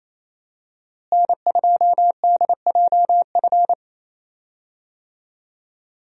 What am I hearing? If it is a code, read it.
Morse code, heard as a keyed tone, about 27 wpm, 700 Hz. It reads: N2DJF
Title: USA Call Signs - 5 Characters - Repeated 3x - 28wpm